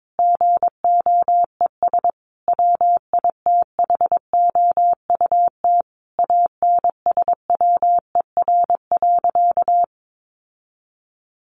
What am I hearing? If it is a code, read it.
Morse code, heard as a keyed tone, about 22 wpm, 700 Hz. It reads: GOES WITHOUT ANSWER.
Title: Sentences from Top 1500 Words 22wpm